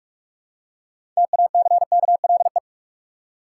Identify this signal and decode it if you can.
Morse code, heard as a keyed tone, about 46 wpm, 700 Hz. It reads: TACKLE
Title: Six Letter Words - 45wpm